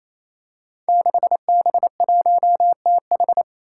Morse code, heard as a keyed tone, about 28 wpm, 700 Hz. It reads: 6B1TH
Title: Five Challenging Letter-Number Combinations 28wpm